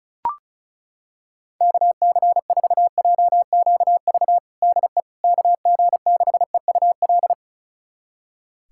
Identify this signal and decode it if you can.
Morse code, heard as a keyed tone, about 36 wpm, 700 Hz. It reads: KC4JQV DE KG6EUL
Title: USA Callsign DE Callsign - 6 Characters - 35wpm